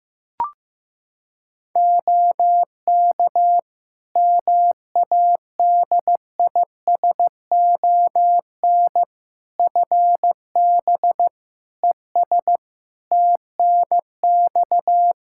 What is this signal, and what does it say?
Morse code, heard as a keyed tone, about 15 wpm, 700 Hz. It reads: OK MADISON FB ES TNX
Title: Practice QSO - Exchange 2 - Info - 15wpm